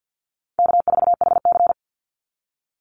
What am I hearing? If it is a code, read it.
Morse code, heard as a keyed tone, about 50 wpm, 700 Hz. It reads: K4HC
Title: Easy Call Signs Repeated 2x - 50wpm